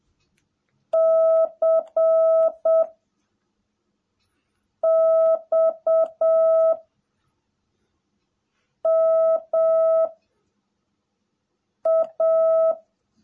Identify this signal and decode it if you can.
Morse code, heard as a keyed tone, about 7 wpm, 660 Hz. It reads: C X M A
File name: jh8cbh / 2021cw / cw18a